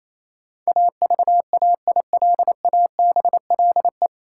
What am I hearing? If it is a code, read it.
Morse code, heard as a keyed tone, about 28 wpm, 700 Hz. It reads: AVAILABLE
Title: Sentences from Top 600 Words - New Word Drill - 28wpm